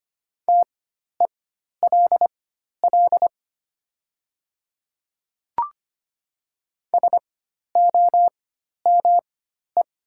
Morse code, heard as a keyed tone, about 25 wpm, spaced as for 12 wpm, 690 Hz. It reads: TELL SOME
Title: 4x Character Spacing - Top 200 Words - 25wpm